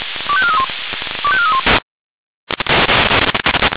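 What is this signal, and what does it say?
Five-tone selective call (ZVEI), heard as quick runs of five tones, tones of about 70 ms. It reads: 25431 25431
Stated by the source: ZVEI five-tone selective call received by SDR.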